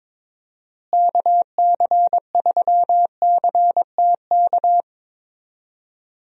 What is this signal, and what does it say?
Morse code, heard as a keyed tone, about 22 wpm, 700 Hz. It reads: KC3CTK
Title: USA Call Signs - 6 Characters - Repeated 3x - 22wpm